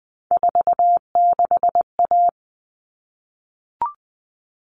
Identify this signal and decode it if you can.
Morse code, heard as a keyed tone, about 20 wpm, 700 Hz. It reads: 46A
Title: All Cut Numbers - 3-Digit 20wpm